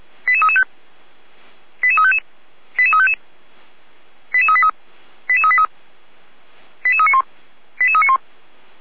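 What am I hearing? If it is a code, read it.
Five-tone selective call (ZVEI), heard as quick runs of five tones, tones of about 70 ms. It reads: 88335 88355 88355 88383 88383 88391 88391
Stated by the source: ZVEI five-tone selective call received on SDR.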